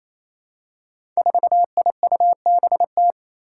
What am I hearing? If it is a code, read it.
Morse code, heard as a keyed tone, about 28 wpm, 700 Hz. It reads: 4IUBT
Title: Five Challenging Letter-Number Combinations 28wpm